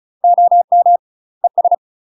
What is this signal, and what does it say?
Morse code, heard as a keyed tone, about 36 wpm, 700 Hz. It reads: OM ES
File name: UK MASTER QSOs TRAINING / 35WPM / UK MASTER QSOs 0041